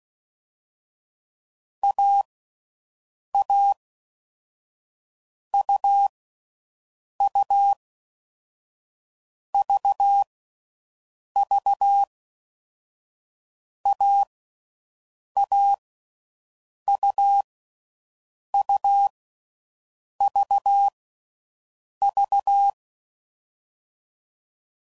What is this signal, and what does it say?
Morse code, heard as a keyed tone, about 16 wpm, 800 Hz. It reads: A A U U V V A A U U V V
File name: Combinations / Difficult Combos / Difficult Combos